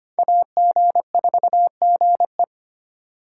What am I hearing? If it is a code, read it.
Morse code, heard as a keyed tone, about 34 wpm, spaced as for 30 wpm, 700 Hz. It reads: AG4GE
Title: Speed Racing - USA 2x2 Call Signs 25wpm